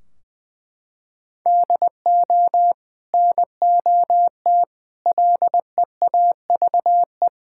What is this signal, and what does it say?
Morse code, heard as a keyed tone, about 20 wpm, 700 Hz. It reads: DO NOT LEAVE
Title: Sentences from Top 1600 Words 20wpm